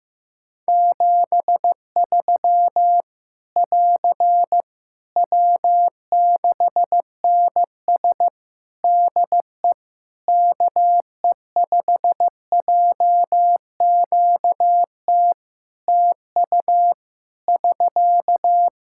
Morse code, heard as a keyed tone, about 15 wpm, 700 Hz. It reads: 73 <AR> W6NS DE KE5JQT TU <SK>
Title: Practice QSO - Exchange 4 - Ending - 15wpm